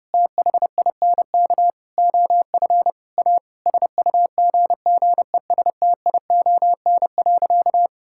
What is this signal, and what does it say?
Morse code, heard as a keyed tone, about 32 wpm, 700 Hz. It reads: THINK OF A SUGGESTION.
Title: Sentences from Top 1600 Words 30wpm